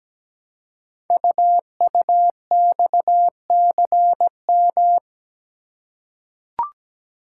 Morse code, heard as a keyed tone, about 17 wpm, 700 Hz. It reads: UUXCM
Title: Code Groups - 5 Letters - 17wpm